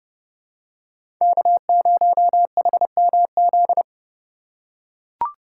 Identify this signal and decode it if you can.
Morse code, heard as a keyed tone, about 30 wpm, 700 Hz. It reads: K0HMZ
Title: K1USN SST QSO Practice - 30wpm